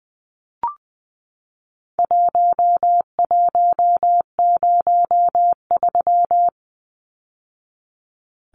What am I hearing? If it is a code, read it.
Morse code, heard as a keyed tone, about 20 wpm, 700 Hz. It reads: 1103
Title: Numbers - Contest Style - 4-Digit 20wpm